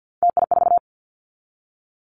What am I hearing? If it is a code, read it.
Morse code, heard as a keyed tone, about 48 wpm, 700 Hz. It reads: TI4
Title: Callsign Prefixes for Top 100 DXCC - 50wpm